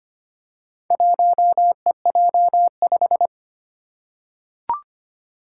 Morse code, heard as a keyed tone, about 25 wpm, 700 Hz. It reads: 1EJ5
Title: Four Challenging Letter-Number Combinations 25wpm